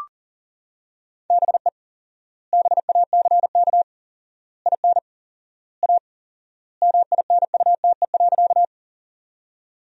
Morse code, heard as a keyed tone, about 37 wpm, 700 Hz. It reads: BE BACK IN A MINUTE.
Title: 4x Word-Spacing - Sentences from Top 500 Words - 40wpm